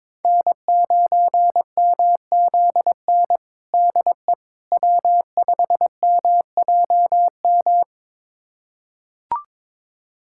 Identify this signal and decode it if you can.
Morse code, heard as a keyed tone, about 22 wpm, 700 Hz. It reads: N9MZN DE W5MJM
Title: USA Callsign DE Callsign - 5 Characters - 22wpm